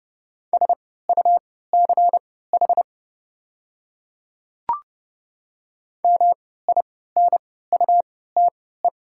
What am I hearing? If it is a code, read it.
Morse code, heard as a keyed tone, about 29 wpm, spaced as for 17 wpm, 700 Hz. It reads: SUCH MINUTE
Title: 3x Character Spacing - Top 500 Words - 30wpm